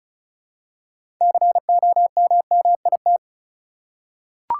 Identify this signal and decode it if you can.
Morse code, heard as a keyed tone, about 36 wpm, 700 Hz. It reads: COMMIT
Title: Six Letter Words - 35wpm